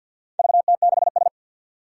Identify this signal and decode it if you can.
Morse code, heard as a keyed tone, about 47 wpm, 700 Hz. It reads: VT6S
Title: Four Challenging Letter-Number Combinations 50wpm